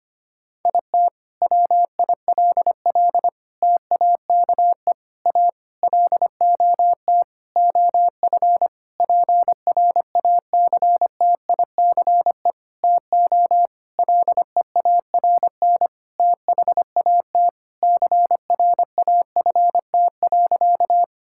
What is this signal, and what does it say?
Morse code, heard as a keyed tone, about 25 wpm, 700 Hz. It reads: IT WILL TAKE A LOT OF PRACTICE TO LEARN THAT CRAFT.
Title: Sentences from Top 1000 Words - Review Part 6 - 25wpm